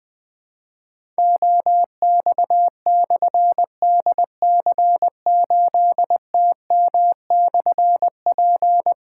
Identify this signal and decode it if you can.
Morse code, heard as a keyed tone, about 20 wpm, 700 Hz. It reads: OX/DC8TM/P
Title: Difficult Call Signs 20wpm